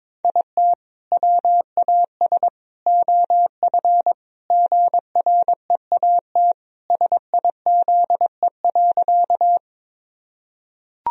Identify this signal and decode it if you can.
Morse code, heard as a keyed tone, about 21 wpm, 700 Hz. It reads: IT WAS OF GREAT SIZE.
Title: Sentences from Top 1000 Words - Review Part 2 - 22wpm